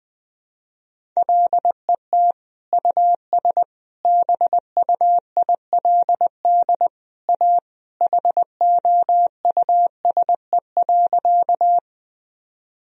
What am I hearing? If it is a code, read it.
Morse code, heard as a keyed tone, about 20 wpm, 700 Hz. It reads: LET US BUILD A HOUSE.
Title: Sentences from Top 1000 Words - Review Part 1 - 20wpm